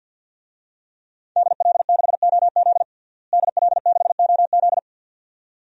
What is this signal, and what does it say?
Morse code, heard as a keyed tone, about 49 wpm, 700 Hz. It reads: DL6OZ DL6OZ
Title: DXCC-100 Call Signs - 5 Characters - Repeated 2x - 50wpm